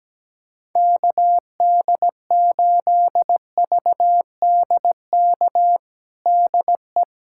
Morse code, heard as a keyed tone, about 17 wpm, 700 Hz. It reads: KD8VDK DE
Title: USA Callsign DE Callsign 17wpm